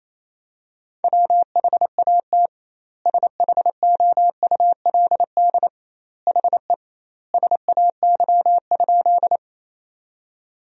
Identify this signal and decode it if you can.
Morse code, heard as a keyed tone, about 27 wpm, 700 Hz. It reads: WHAT SHOULD HE SAY?
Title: 2x Word-Spacing - Sentences from Top 300 Words - 28wpm